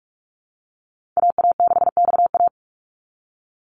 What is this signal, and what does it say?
Morse code, heard as a keyed tone, about 44 wpm, 700 Hz. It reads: AA6XA
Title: NA SOTA Activator Callsigns 45wpm